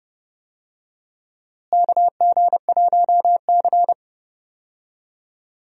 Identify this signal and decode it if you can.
Morse code, heard as a keyed tone, about 30 wpm, 700 Hz. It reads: KG1C
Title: USA Call Signs - 4 Characters - Repeated 3x - 30wpm